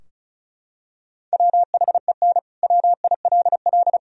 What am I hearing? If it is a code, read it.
Morse code, heard as a keyed tone, about 36 wpm, 700 Hz. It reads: WHEN WILL
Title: Sentences from Top 1000 Words - Review Part 4 - 35wpm